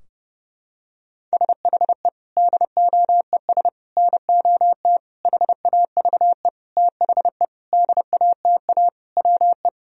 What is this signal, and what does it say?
Morse code, heard as a keyed tone, about 30 wpm, 700 Hz. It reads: SHE DOES NOT HAVE THE DATA WE
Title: Sentences from Top 1000 Words - Review Part 3 - 30wpm